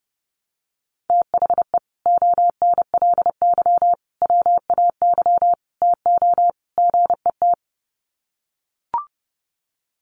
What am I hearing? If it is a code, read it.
Morse code, heard as a keyed tone, about 30 wpm, 700 Hz. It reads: THE ONLY WAY TO GET
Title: Sets of 5 Words Encore2 30wpm